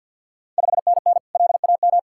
Morse code, heard as a keyed tone, about 49 wpm, 700 Hz. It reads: 5NN LAM
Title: Ontario QSO Party Exchange 50wpm